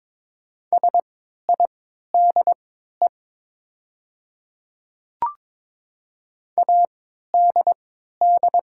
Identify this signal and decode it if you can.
Morse code, heard as a keyed tone, about 22 wpm, spaced as for 13 wpm, 700 Hz. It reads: SIDE ADD
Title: 3x Character Spacing - Top 200 Words - 22wpm